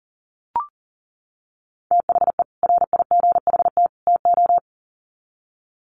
Morse code, heard as a keyed tone, about 40 wpm, 700 Hz. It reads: THE RIGHT TO
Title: Sets of 3 Words 40wpm